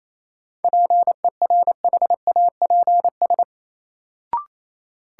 Morse code, heard as a keyed tone, about 28 wpm, 700 Hz. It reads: PERHAPS
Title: Sentences from Top 500 Words - New Word Drill - 28wpm